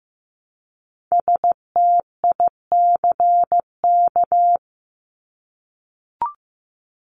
Morse code, heard as a keyed tone, about 15 wpm, 700 Hz. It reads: STICK
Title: Top 1001-1250 Words 15wpm